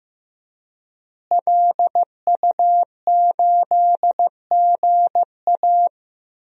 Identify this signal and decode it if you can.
Morse code, heard as a keyed tone, about 16 wpm, 700 Hz. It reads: LU8GA
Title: DXCC-100 Call Signs - 5 Characters - Repeated 2x - 15wpm